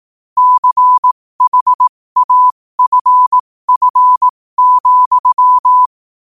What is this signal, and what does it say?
Morse code, heard as a keyed tone, about 18 wpm, spaced as for 15 wpm, 1000 Hz. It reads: CHAFF,
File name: qotdfiles / 15wpm